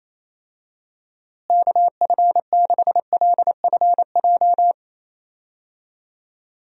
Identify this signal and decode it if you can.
Morse code, heard as a keyed tone, about 28 wpm, 700 Hz. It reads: KF6LFJ
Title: USA Call Signs - 6 Characters - Repeated 2x - 28wpm